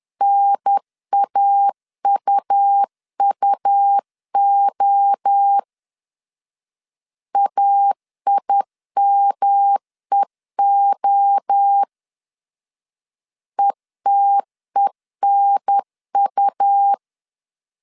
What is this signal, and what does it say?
Morse code, heard as a keyed tone, about 10 wpm, 800 Hz. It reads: NAUUO AIMEO ETENU
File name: CW / DET / 1 3 1500C S C 10wpm 3335